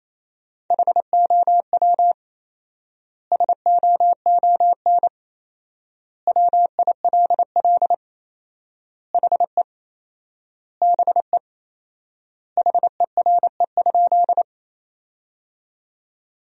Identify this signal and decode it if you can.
Morse code, heard as a keyed tone, about 28 wpm, 700 Hz. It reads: HOW SOON WILL HE BE HERE?
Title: 4x Word-Spacing - Sentences from Top 400 Words - 28wpm